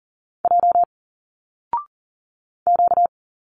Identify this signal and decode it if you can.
Morse code, heard as a keyed tone, about 41 wpm, 700 Hz. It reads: J Q
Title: Single Letters 40wpm